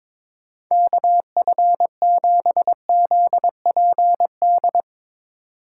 Morse code, heard as a keyed tone, about 22 wpm, 700 Hz. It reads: KF7ZPD
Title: USA Call Signs - 6 Characters - 2x Character Spacing - 22wpm